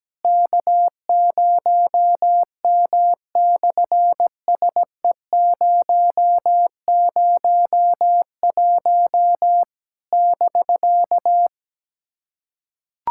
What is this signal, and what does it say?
Morse code, heard as a keyed tone, about 17 wpm, 700 Hz. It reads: K0M/SE001 <BK>
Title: SOTA QSO Practice 17wpm